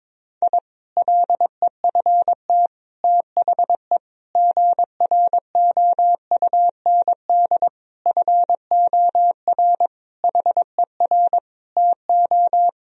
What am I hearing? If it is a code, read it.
Morse code, heard as a keyed tone, about 22 wpm, 700 Hz. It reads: I LEFT THE GROUND FOR HER TO
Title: Sentences from Top 1000 Words - Review Part 2 - 22wpm